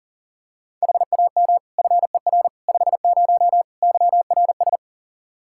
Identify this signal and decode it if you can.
Morse code, heard as a keyed tone, about 40 wpm, 700 Hz. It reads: HAM FER 50 YRS
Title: Practice QSO - Exchange 3 - More Info - 40wpm